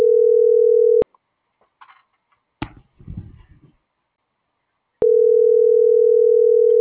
Ringback tone (North America), 440 + 480 Hz.